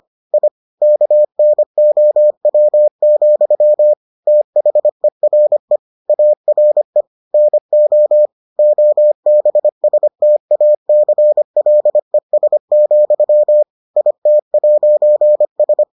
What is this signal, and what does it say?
Morse code, heard as a keyed tone, about 25 wpm, 600 Hz. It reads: I KNOW, THERE ARE NO OBSTACLES, IT'S